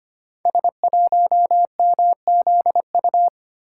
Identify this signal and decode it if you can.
Morse code, heard as a keyed tone, about 25 wpm, 700 Hz. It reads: S1MZU
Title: Code Groups - 5 Letter:Numbers - 25wpm